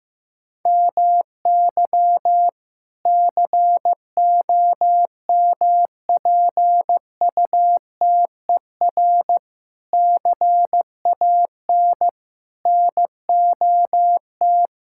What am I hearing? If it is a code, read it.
Morse code, heard as a keyed tone, about 15 wpm, 700 Hz. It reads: MY COMPUTER CAN NOT
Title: Sentences from Top 1000 Words - Review Part 6 - 15wpm